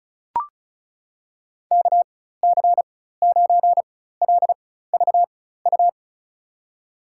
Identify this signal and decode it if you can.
Morse code, heard as a keyed tone, about 35 wpm, spaced as for 16 wpm, 700 Hz. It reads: KC9LVU
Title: USA Call Signs - 4x Character Spacing - 35wpm